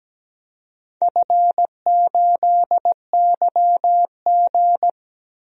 Morse code, heard as a keyed tone, about 17 wpm, 700 Hz. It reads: F8YG
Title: DXCC-100 Call Signs - 4 Characters - Repeated 2x - 17wpm